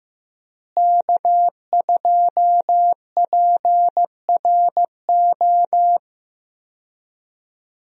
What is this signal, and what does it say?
Morse code, heard as a keyed tone, about 15 wpm, 700 Hz. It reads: K2PRO
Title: USA Call Signs - 5 Characters - Repeated 3x - 15wpm